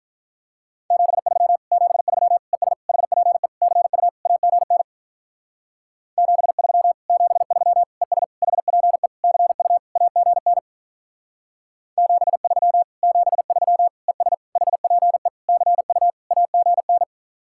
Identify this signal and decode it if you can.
Morse code, heard as a keyed tone, about 44 wpm, 700 Hz. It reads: 73 73 ES HPE CU AGN 73 73 ES HPE CU AGN 73 73 ES HPE CU AGN